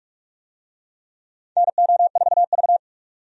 Morse code, heard as a keyed tone, about 46 wpm, 700 Hz. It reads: NO4V
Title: USA Call Signs - 4 Characters - 45wpm